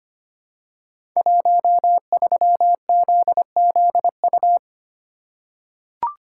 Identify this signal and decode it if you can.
Morse code, heard as a keyed tone, about 25 wpm, 700 Hz. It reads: 13ZZU
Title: Code Groups - 5 Letter:Numbers - 25wpm